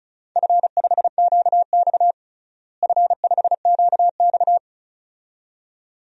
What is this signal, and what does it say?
Morse code, heard as a keyed tone, about 34 wpm, 700 Hz. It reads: F5QX F5QX
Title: DXCC-100 Call Signs - 4 Characters - Repeated 2x - 35wpm